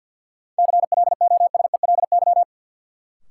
Morse code, heard as a keyed tone, about 53 wpm, 700 Hz. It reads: CLOSELY
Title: Sentences from Top 1400 Words - New Word Drill Alt - 50wpm